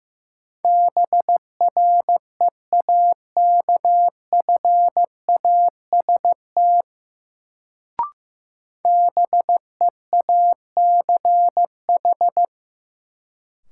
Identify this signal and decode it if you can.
Morse code, heard as a keyed tone, about 15 wpm, 700 Hz. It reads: BREAKFAST BEACH
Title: Sentences from Top 1500 Words - New Word Drill Alt - 15wpm